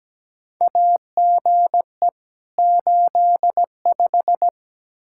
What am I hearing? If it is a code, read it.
Morse code, heard as a keyed tone, about 17 wpm, 700 Hz. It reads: AGE 85
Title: Practice QSO - Exchange 3 - More Info - 17wpm